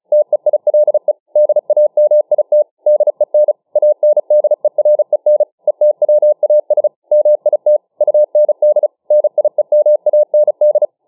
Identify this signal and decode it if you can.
Morse code, heard as a keyed tone, about 37 wpm, 600 Hz. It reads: TEILE DAMIT DEN ANDEREN ETWAS MIT UND NIEMAND